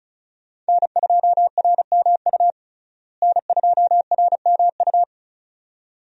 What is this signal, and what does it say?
Morse code, heard as a keyed tone, about 36 wpm, 700 Hz. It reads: N2RMU N2RMU
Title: USA Call Signs - 5 Characters - Repeated 2x - 35wpm